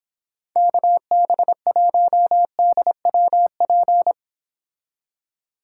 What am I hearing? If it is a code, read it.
Morse code, heard as a keyed tone, about 26 wpm, 700 Hz. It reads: KB1DWP